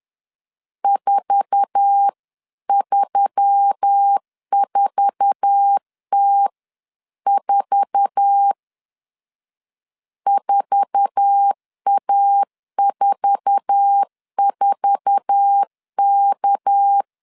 Morse code, heard as a keyed tone, about 10 wpm, 800 Hz. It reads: 434T4 4A44K
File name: LEZIONI / 8 YJ45 / 8 YJ4 10wpm 2633